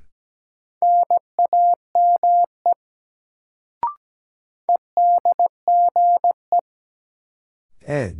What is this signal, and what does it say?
Morse code, heard as a keyed tone, about 17 wpm, 700 Hz. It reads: NAME EDGE
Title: Sets of 3 Words - New Word Drill - 17wpm